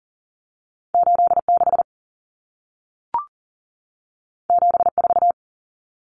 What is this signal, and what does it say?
Morse code, heard as a keyed tone, about 40 wpm, 700 Hz. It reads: 86 74
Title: Common Cut Numbers - 2-Digit 40wpm